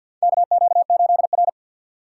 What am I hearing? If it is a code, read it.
Morse code, heard as a keyed tone, about 52 wpm, 700 Hz. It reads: KQ8R
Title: USA Call Signs - 4x Character Spacing - 50wpm